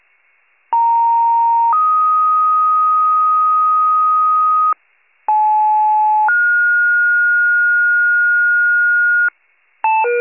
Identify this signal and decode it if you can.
Two-tone page, heard as a one-second tone, then a three-second tone, about 930 then 1250 Hz, about 860 then 1430 Hz, with noise. It is heard from 0.7 to 9.3 s.
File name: pager / 82 FIRE / wcalert